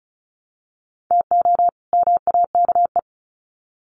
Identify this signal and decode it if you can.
Morse code, heard as a keyed tone, about 34 wpm, 700 Hz. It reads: TO MAKE
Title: Sets of 2 Words 35wpm